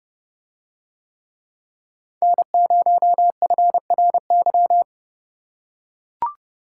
Morse code, heard as a keyed tone, about 29 wpm, 700 Hz. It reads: N0FRY